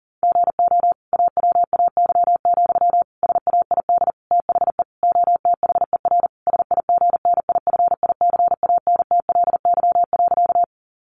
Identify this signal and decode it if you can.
Morse code, heard as a keyed tone, about 40 wpm, 700 Hz. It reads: GO AWAY, SAID THE OTHER SIGNIFICANTLY.
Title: Sentences from Top 1500 Words 40wpm